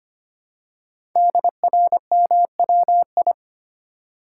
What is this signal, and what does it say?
Morse code, heard as a keyed tone, about 25 wpm, 700 Hz. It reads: DRMWI
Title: Code Groups - 5 Letters - 25wpm